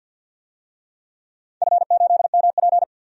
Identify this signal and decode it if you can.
Morse code, heard as a keyed tone, about 51 wpm, 700 Hz. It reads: F8MP
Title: DXCC-100 Call Signs - 4 Characters - 50wpm